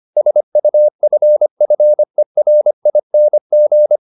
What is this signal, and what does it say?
Morse code, heard as a keyed tone, about 25 wpm, 600 Hz. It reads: SUFFERING